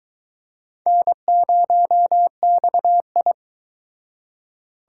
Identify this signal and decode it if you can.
Morse code, heard as a keyed tone, about 24 wpm, 700 Hz. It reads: N0XI